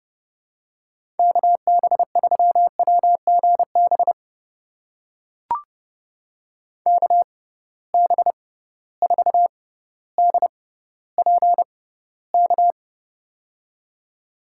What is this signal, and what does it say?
Morse code, heard as a keyed tone, about 30 wpm, spaced as for 15 wpm, 700 Hz. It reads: KB3WGB KB4DPK
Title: USA Call Signs - 6 Characters - 6x Character Spacing - 30wpm